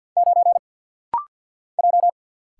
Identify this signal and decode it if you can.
Morse code, heard as a keyed tone, about 50 wpm, 700 Hz. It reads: 9 J
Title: Single Letter-Number - Rapid-Fire 50wpm